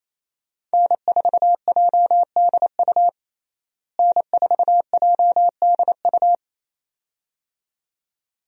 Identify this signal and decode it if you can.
Morse code, heard as a keyed tone, about 28 wpm, 700 Hz. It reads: N4JDU N4JDU
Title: USA Call Signs - 5 Characters - Repeated 2x - 28wpm